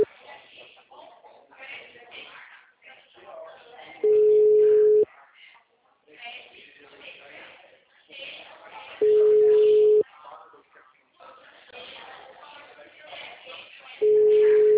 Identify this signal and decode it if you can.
Ringback tone (Europe, 425 Hz), with speech.